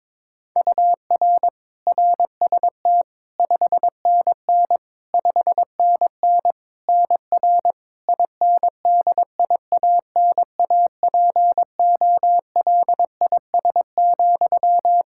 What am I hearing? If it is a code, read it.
Morse code, heard as a keyed tone, about 22 wpm, 700 Hz. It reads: UR RST 5NN 5NN NR INDIANAPOLIS,
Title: K1USN SST QSO Practice 22wpm